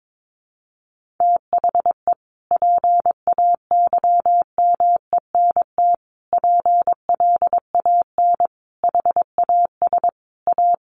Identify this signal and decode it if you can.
Morse code, heard as a keyed tone, about 22 wpm, 700 Hz. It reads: THE PAYMENT PLAN HAS A
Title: Sentences from Top 1500 Words 22wpm